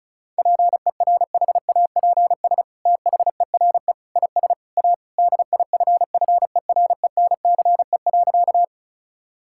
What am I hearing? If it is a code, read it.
Morse code, heard as a keyed tone, about 36 wpm, 700 Hz. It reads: PERHAPS THERE IS A DIFFERENCE.
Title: Sentences from Top 1000 Words - Review Part 5 - 35wpm